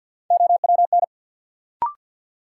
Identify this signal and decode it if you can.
Morse code, heard as a keyed tone, about 53 wpm, 700 Hz. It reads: OWN
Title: Sentences from Top 300 Words - New Word Drill - 50wpm